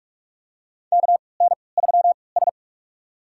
Morse code, heard as a keyed tone, about 45 wpm, spaced as for 25 wpm, 700 Hz. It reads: KN3S
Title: USA Call Signs - 4 Characters - 3x Character Spacing - 45wpm